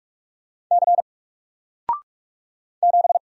Morse code, heard as a keyed tone, about 44 wpm, 700 Hz. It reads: C 7
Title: Single Character (Letter:Number:Punctuation:Prosign) 45wpm